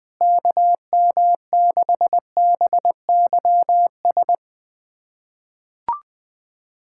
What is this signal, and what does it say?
Morse code, heard as a keyed tone, about 20 wpm, 690 Hz. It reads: KM6BYS
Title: SOTA QSO Practice 20wpm